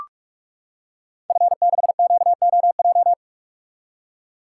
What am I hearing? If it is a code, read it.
Morse code, heard as a keyed tone, about 44 wpm, 700 Hz. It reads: F6QOJ
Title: Code Groups - 5 Letter:Numbers - 45wpm